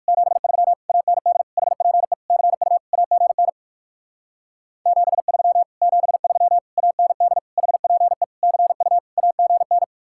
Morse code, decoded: 73 AND HPE CU AGN 73 73 AND HPE CU AGN

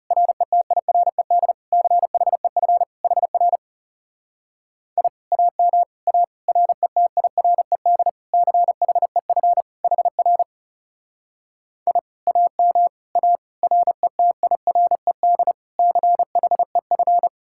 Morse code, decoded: RETIRED CHEF HR I AM A RETIRED CHEF HR I AM A RETIRED CHEF

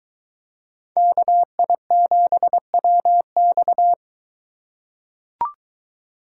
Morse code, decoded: KI7WX

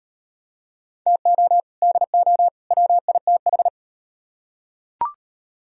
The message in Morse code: TO DO WITH

